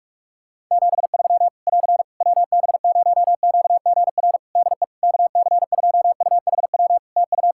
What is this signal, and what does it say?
Morse code, heard as a keyed tone, about 41 wpm, 700 Hz. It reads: 73 <AR> WB0QGR DE KC2UHW TU
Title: Practice QSO - Exchange 4 - Ending - 45wpm